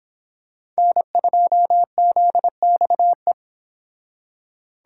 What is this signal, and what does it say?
Morse code, heard as a keyed tone, about 26 wpm, 700 Hz. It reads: N2ZXE